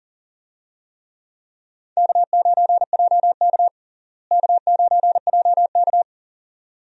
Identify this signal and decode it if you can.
Morse code, heard as a keyed tone, about 37 wpm, 700 Hz. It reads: K9JK K9JK
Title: USA Call Signs - 4 Characters - Repeated 2x - 40wpm